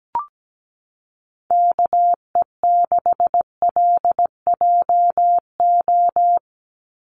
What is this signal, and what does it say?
Morse code, heard as a keyed tone, about 17 wpm, 700 Hz. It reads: KE6LJO
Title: USA Call Signs 17wpm